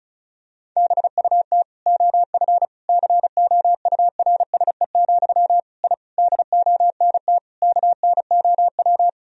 Morse code, heard as a keyed tone, about 34 wpm, 700 Hz. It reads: BUT OF COURSE, I DONT KNOW
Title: Sentences from Top 1000 Words - Review Part 2 - 35wpm